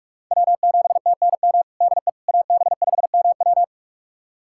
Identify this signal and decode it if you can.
Morse code, heard as a keyed tone, about 46 wpm, 700 Hz. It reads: W7TNM DE AB5MW
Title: USA Callsign DE Callsign - 5 Characters - 45wpm